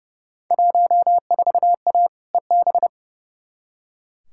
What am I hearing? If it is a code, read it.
Morse code, heard as a keyed tone, about 30 wpm, 700 Hz. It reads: 14A EB
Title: Field Day Exchanges Encore - 30wpm